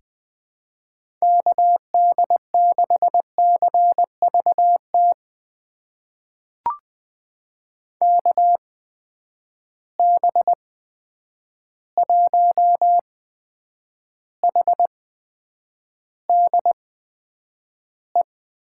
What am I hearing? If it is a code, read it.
Morse code, heard as a keyed tone, about 20 wpm, spaced as for 9 wpm, 700 Hz. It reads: KD6CVT KB1HDE
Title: USA Call Signs - 6 Characters - 8x Character Spacing - 20wpm